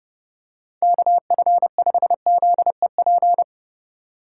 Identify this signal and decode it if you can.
Morse code, heard as a keyed tone, about 30 wpm, 700 Hz. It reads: KF5ZEP
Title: USA Call Signs - 6 Characters - 2x Character Spacing - 30wpm